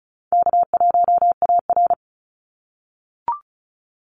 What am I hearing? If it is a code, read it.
Morse code, heard as a keyed tone, about 34 wpm, 700 Hz. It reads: K1AR
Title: Call Signs - 35wpm